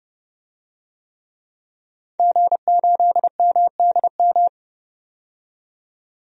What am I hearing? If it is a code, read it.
Morse code, heard as a keyed tone, about 30 wpm, 700 Hz. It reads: G8MDM
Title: DXCC-100 Call Signs - 5 Characters - 30wpm